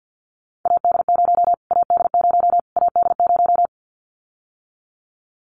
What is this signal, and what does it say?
Morse code, heard as a keyed tone, about 48 wpm, 700 Hz. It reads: AD0 AD0 AD0
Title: Callsign Prefixes for Top 25 DXCC - Repeated 3x - 50wpm